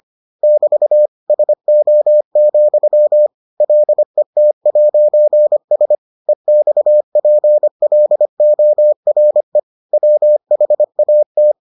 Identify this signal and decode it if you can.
Morse code, heard as a keyed tone, about 25 wpm, 600 Hz. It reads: <BT> SO, LET'S EXPLORE WHAT